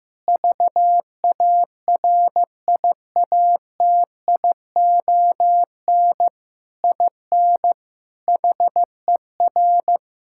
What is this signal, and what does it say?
Morse code, heard as a keyed tone, about 15 wpm, 700 Hz. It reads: VARIATION IN HER
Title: Sentences from Top 1600 Words 15wpm